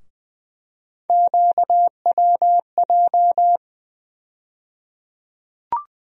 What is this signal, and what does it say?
Morse code, heard as a keyed tone, about 20 wpm, 700 Hz. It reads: QWJ
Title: USA Call Sign Postfixes - 3 Letters 20wpm